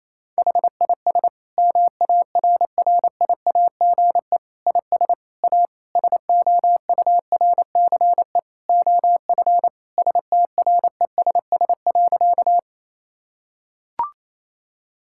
Morse code, heard as a keyed tone, about 28 wpm, 700 Hz. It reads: HIS MARRIAGE IS A SOURCE OF STRESS.
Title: Sentences from Top 1000 Words - Review Part 6 - 28wpm